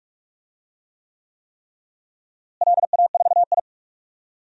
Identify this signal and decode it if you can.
Morse code, heard as a keyed tone, about 46 wpm, 700 Hz. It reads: LA4I
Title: DXCC-100 Call Signs - 4 Characters - Repeated 3x - 45wpm